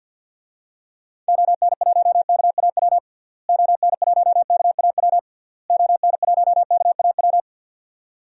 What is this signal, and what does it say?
Morse code, heard as a keyed tone, about 51 wpm, 700 Hz. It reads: ON1KAW ON1KAW ON1KAW
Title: DXCC-100 Call Signs - 6 Characters - Repeated 3x - 50wpm